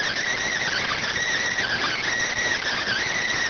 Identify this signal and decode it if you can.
Emergency Alert System (SAME) tones, heard as harsh buzzing bursts. It is heard from 0.6 to 3.3 s.